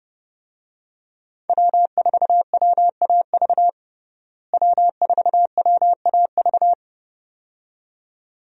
Morse code, heard as a keyed tone, about 30 wpm, 700 Hz. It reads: W4WAV W4WAV
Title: USA Call Signs - 5 Characters - Repeated 2x - 30wpm